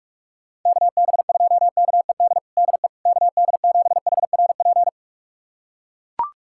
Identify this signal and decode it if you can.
Morse code, heard as a keyed tone, about 41 wpm, 700 Hz. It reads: KB2KED DE KD7HRP
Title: USA Callsign DE Callsign 45wpm